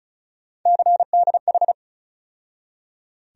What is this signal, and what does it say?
Morse code, heard as a keyed tone, about 36 wpm, 700 Hz. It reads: CDH
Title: USA Call Sign Postfixes - 3 Letters 35wpm